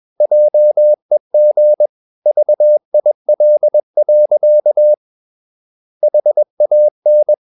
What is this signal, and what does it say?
Morse code, heard as a keyed tone, about 21 wpm, 600 Hz. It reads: JEG VIL. HAN
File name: Ferie QRS 21wpm